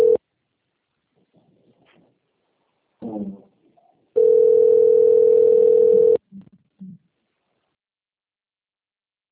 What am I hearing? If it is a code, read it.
Ringback tone (North America), 440 + 480 Hz.